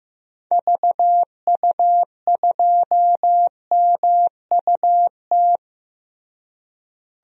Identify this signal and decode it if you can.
Morse code, heard as a keyed tone, about 15 wpm, 700 Hz. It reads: VU2MUT
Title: DXCC-100 Call Signs - 6 Characters - Repeated 2x - 15wpm